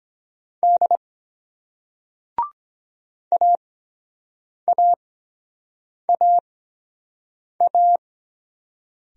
Morse code, heard as a keyed tone, about 23 wpm, spaced as for 7 wpm, 700 Hz. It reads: D AAAA